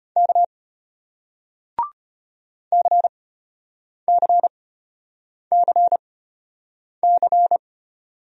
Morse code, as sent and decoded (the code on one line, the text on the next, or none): -.- / -.-. -.-. -.-. -.-.
K CCCC